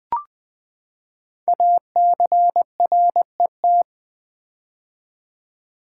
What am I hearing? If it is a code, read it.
Morse code, heard as a keyed tone, about 20 wpm, 700 Hz. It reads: ACRET
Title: Code Groups - 5 Letters - 20wpm